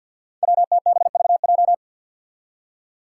Morse code, heard as a keyed tone, about 51 wpm, 700 Hz. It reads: WTBVJ
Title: Five Challenging Letter-Number Combinations 50wpm